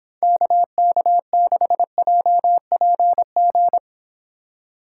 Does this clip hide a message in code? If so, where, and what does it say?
Yes: Morse code from 0.2 to 3.8 s, heard as a keyed tone, about 26 wpm, 700 Hz. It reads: KK6JPG